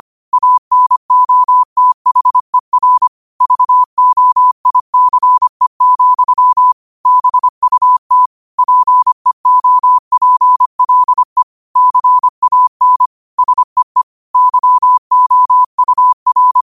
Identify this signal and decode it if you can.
Morse code, heard as a keyed tone, about 25 wpm, 1000 Hz. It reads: ANOTHER VOICE, BUT PEOPLE CAN SEE YOUR
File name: qotdfiles / 25wpm